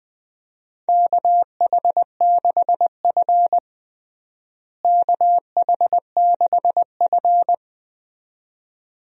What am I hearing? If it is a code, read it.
Morse code, heard as a keyed tone, about 20 wpm, 700 Hz. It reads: KH6F KH6F
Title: USA Call Signs - 4 Characters - Repeated 2x - 20wpm